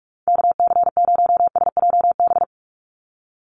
Morse code, heard as a keyed tone, about 44 wpm, 700 Hz. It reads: KC0SJB
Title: USA Call Signs 45wpm